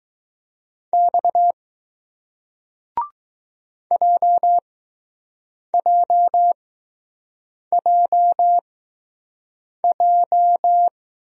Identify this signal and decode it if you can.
Morse code, heard as a keyed tone, about 20 wpm, spaced as for 6 wpm, 700 Hz. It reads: X JJJJ